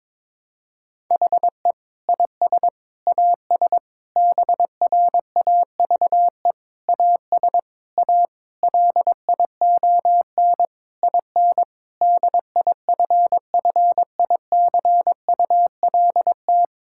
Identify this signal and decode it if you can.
Morse code, heard as a keyed tone, about 21 wpm, 700 Hz. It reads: HE IS AS BRAVE AS A LION IN DIFFICULT
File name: Difficult Sentences with Hints - 22wpm